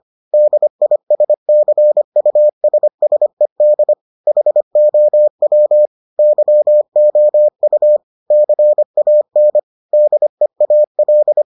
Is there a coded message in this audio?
Yes: Morse code from 0.3 to 11.4 s, heard as a keyed tone, about 25 wpm, 600 Hz. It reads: DISCUSSED HOW YOU CAN DEAL